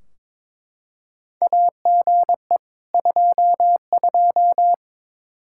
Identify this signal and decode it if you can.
Morse code, heard as a keyed tone, about 22 wpm, 700 Hz. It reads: AGE 22
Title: Practice QSO - Exchange 3 - More Info - 22wpm